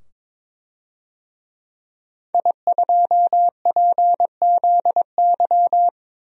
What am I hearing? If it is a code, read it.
Morse code, heard as a keyed tone, about 22 wpm, 700 Hz. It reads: I2PZY
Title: DXCC-100 Call Signs - 5 Characters - 22wpm